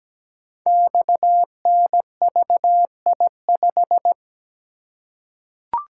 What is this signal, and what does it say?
Morse code, heard as a keyed tone, about 17 wpm, 700 Hz. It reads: XNVI5
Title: Code Groups - 5 Letter:Numbers - 17wpm